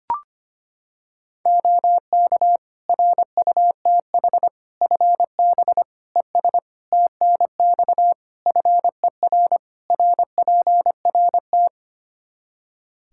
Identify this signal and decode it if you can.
Morse code, heard as a keyed tone, about 25 wpm, 700 Hz. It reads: OK RUTH FB ES TNX FER RPRT
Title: Practice QSO - Exchange 2 - Info - 25wpm